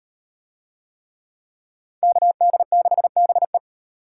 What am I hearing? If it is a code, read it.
Morse code, heard as a keyed tone, about 38 wpm, 700 Hz. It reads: KD6BE